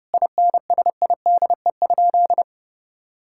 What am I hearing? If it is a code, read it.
Morse code, heard as a keyed tone, about 30 wpm, 700 Hz. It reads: INSIDE?
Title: Sentences from Top 1000 Words - Review Part 6 - 30wpm